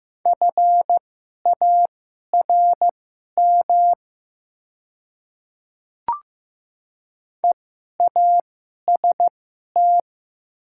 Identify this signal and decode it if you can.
Morse code, heard as a keyed tone, about 15 wpm, spaced as for 11 wpm, 700 Hz. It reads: FARM EAST
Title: 2x Character Spacing - Top 500 Words - 15wpm